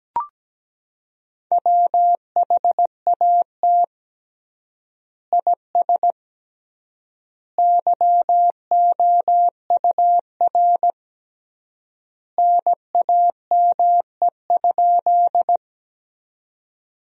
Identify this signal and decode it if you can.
Morse code, heard as a keyed tone, about 17 wpm, 700 Hz. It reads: WHAT IS YOUR NAME?
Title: 3x Word-Spacing - Sentences from Top 200 Words - 17wpm